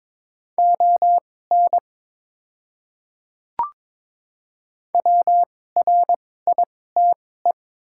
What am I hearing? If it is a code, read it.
Morse code, heard as a keyed tone, about 22 wpm, spaced as for 16 wpm, 700 Hz. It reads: ON WRITE
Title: 2x Character Spacing - Top 100 Words - 22wpm